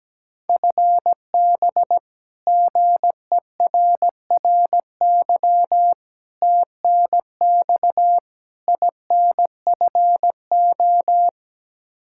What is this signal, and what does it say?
Morse code, heard as a keyed tone, about 17 wpm, 700 Hz. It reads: FB GERRY TNX INFO
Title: Practice QSO - Exchange 3 - More Info - 17wpm